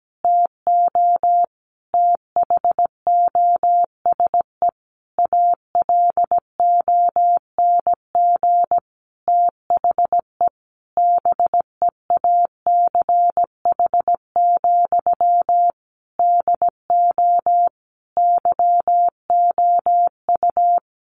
Morse code, heard as a keyed tone, about 17 wpm, 700 Hz. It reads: TO THOSE ALONG THE BEACH, DO YOU
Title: Sentences from Top 1500 Words 17wpm